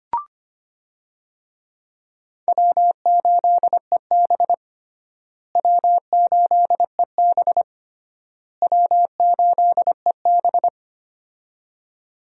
Morse code, heard as a keyed tone, about 25 wpm, 700 Hz. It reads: W8EB W8EB W8EB
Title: USA Call Signs - 4 Characters - Repeated 3x - 25wpm